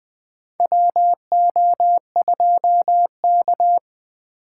Morse code, heard as a keyed tone, about 20 wpm, 700 Hz. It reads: WO2K
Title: USA Call Signs - 4 Characters - 6x Character Spacing - 20wpm